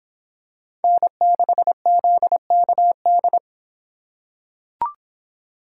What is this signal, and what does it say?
Morse code, heard as a keyed tone, about 26 wpm, 700 Hz. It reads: N6ZKD